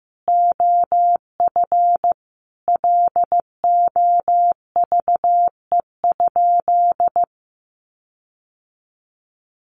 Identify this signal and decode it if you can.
Morse code, heard as a keyed tone, about 15 wpm, 700 Hz. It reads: OF LOVE?
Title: Sentences from Top 1300 Words 15wpm